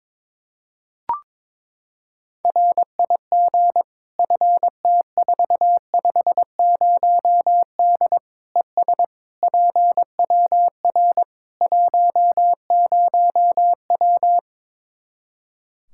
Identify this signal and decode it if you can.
Morse code, heard as a keyed tone, about 21 wpm, 700 Hz. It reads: RIG FT450D ES PWR 10W
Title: Practice QSO - Exchange 2 - Info - 22wpm